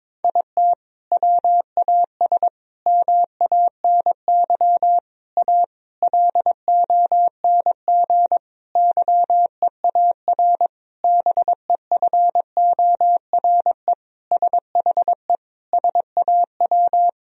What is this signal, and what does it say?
Morse code, heard as a keyed tone, about 22 wpm, 700 Hz. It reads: IT WAS MANY A LONG YEAR BEFORE SHE SAW
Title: Sentences from Top 1000 Words - Review Part 1 - 22wpm